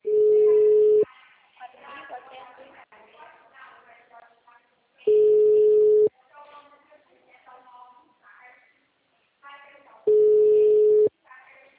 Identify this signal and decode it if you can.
Ringback tone (Europe, 425 Hz), with speech.
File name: out- -2026